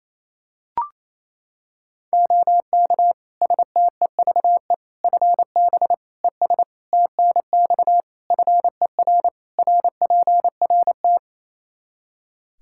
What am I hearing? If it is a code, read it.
Morse code, heard as a keyed tone, about 28 wpm, 700 Hz. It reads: OK STEVE FB ES TNX FER RPRT
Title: Practice QSO - Exchange 2 - Info - 28wpm